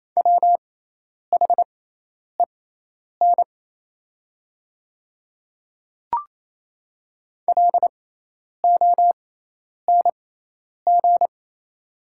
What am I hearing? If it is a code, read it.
Morse code, heard as a keyed tone, about 28 wpm, spaced as for 10 wpm, 700 Hz. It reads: WHEN LONG
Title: 6x Character Spacing - Top 100 Words - 28wpm